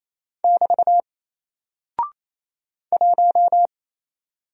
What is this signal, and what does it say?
Morse code, heard as a keyed tone, about 28 wpm, 700 Hz. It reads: <BT> 1
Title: Single Character (Letter:Number:Punctuation:Prosign) Alt 28wpm